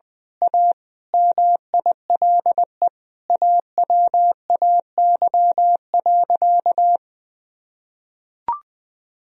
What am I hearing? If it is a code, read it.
Morse code, heard as a keyed tone, about 20 wpm, 700 Hz. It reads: A MILE AWAY.
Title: Sentences from Top 1000 Words - Review Part 2 - 20wpm